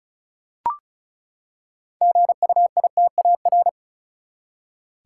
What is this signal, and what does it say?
Morse code, heard as a keyed tone, about 36 wpm, 700 Hz. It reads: GUITAR
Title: Six Letter Words - 35wpm